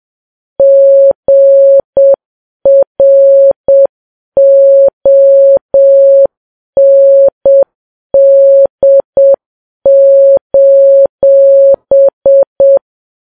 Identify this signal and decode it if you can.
Morse code, heard as a keyed tone, about 7 wpm, 550 Hz. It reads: GROND:
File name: Files Web tls / Morse on9fv / morse7kps